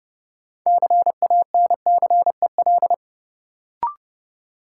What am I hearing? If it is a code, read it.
Morse code, heard as a keyed tone, about 30 wpm, 700 Hz. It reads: CANCEL
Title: Sentences from Top 1200 Words - New Word Drill - 30wpm